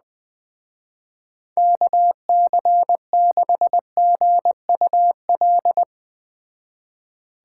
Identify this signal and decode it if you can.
Morse code, heard as a keyed tone, about 20 wpm, 700 Hz. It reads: KC6GUL